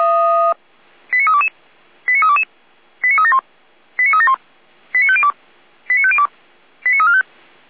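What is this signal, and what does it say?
Five-tone selective call (ZVEI), heard as quick runs of five tones, tones of about 80 ms. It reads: 89322 89322 89371 89371 89502 89502 89345
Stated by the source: ZVEI five-tone call 89305 received on SDR, Feuerwehr Orlamünde server.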